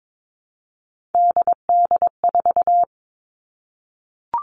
Morse code, heard as a keyed tone, about 22 wpm, 700 Hz. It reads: DD4
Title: Three Challenging Letter-Number Combinations 22wpm